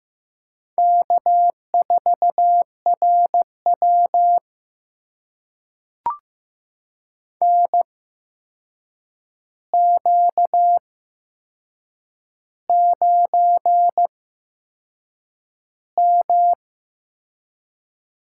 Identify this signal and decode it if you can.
Morse code, heard as a keyed tone, about 15 wpm, spaced as for 6 wpm, 700 Hz. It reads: K4RW NQ9M
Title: USA Call Signs - 4 Characters - 8x Character Spacing - 15wpm